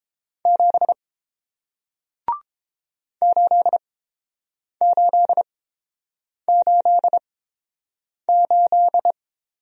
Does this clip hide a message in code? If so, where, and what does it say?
Yes: Morse code from 0.5 to 9.1 s, heard as a keyed tone, about 30 wpm, spaced as for 7 wpm, 700 Hz. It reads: 7 8888